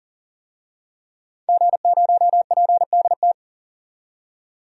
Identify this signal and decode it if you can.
Morse code, heard as a keyed tone, about 40 wpm, 700 Hz. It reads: G0PDT
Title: DXCC-100 Call Signs - 5 Characters - 40wpm